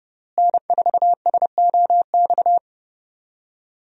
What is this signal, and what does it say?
Morse code, heard as a keyed tone, about 32 wpm, 700 Hz. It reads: N4SOX
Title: USA Call Signs - 5 Characters - Repeated 3x - 30wpm